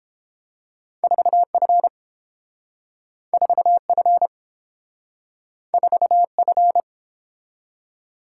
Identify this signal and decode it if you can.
Morse code, heard as a keyed tone, about 30 wpm, 700 Hz. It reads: 4F 4F 4F